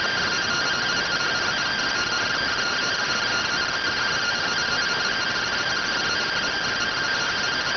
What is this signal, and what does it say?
Two-tone teleprinter signal, about 170 Hz shift.